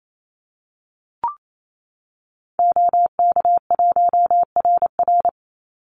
Morse code, heard as a keyed tone, about 28 wpm, 700 Hz. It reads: OK1RR
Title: Call Signs Repeated 2x - 28wpm